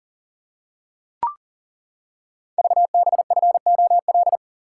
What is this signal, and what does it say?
Morse code, heard as a keyed tone, about 40 wpm, 700 Hz. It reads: VBFOL
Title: Code Groups - 5 Letters - 40wpm